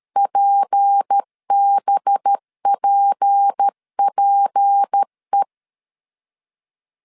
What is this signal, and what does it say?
Morse code, heard as a keyed tone, about 13 wpm, 800 Hz. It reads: PBPPE
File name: LEZIONI / 6 PKX0 / 6 P 12wpm 2033